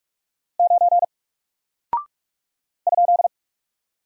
Morse code, heard as a keyed tone, about 44 wpm, 700 Hz. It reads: 9 ?
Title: Single Character (Letter:Number:Punctuation:Prosign) Alt 45wpm